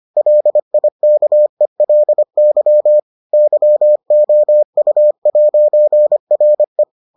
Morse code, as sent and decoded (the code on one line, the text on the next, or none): .-.. .. -.- . .-.. -.-- / -.-- --- ..- .----. .-. .
LIKELY YOU'RE